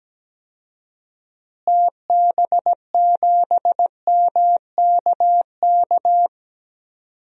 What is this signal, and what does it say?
Morse code, heard as a keyed tone, about 17 wpm, 700 Hz. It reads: TB7MKK
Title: DXCC-100 Call Signs - 6 Characters - Repeated 2x - 17wpm